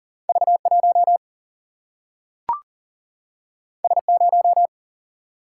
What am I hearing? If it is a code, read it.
Morse code, heard as a keyed tone, about 40 wpm, 700 Hz. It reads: V1 S0
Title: Single-Letter Single-Number 40wpm